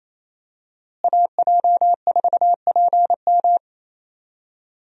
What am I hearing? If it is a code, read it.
Morse code, heard as a keyed tone, about 28 wpm, 700 Hz. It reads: AJ4PM
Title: USA Call Signs - 5 Characters - Repeated 3x - 28wpm